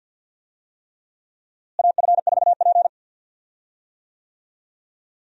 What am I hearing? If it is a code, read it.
Morse code, heard as a keyed tone, about 51 wpm, 700 Hz. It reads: AF4P
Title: USA Call Signs - 4 Characters - Repeated 3x - 50wpm